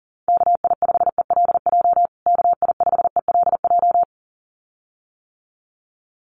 Morse code, heard as a keyed tone, about 40 wpm, 700 Hz. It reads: KI5ELJ KI5ELJ
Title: USA Call Signs Repeated 2x 40wpm